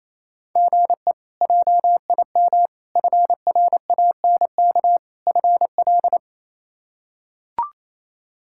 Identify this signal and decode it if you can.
Morse code, heard as a keyed tone, about 28 wpm, 700 Hz. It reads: GE JIM FRANK FL
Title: K1USN SST QSO Practice - 28wpm